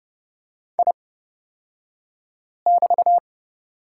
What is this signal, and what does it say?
Morse code, heard as a keyed tone, about 30 wpm, spaced as for 5 wpm, 700 Hz. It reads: I<BT>
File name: Lesson 36d - BT - Two Characters 30wpm